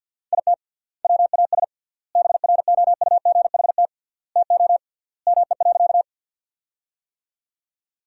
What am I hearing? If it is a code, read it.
Morse code, heard as a keyed tone, about 47 wpm, 700 Hz. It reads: IT WAS BROUGHT TO ME.
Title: 3x Word-Spacing - Sentences from Top 500 Words - 50wpm